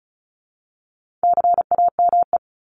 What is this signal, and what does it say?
Morse code, heard as a keyed tone, about 34 wpm, 700 Hz. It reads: CAME
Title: Top 300 Words 35wpm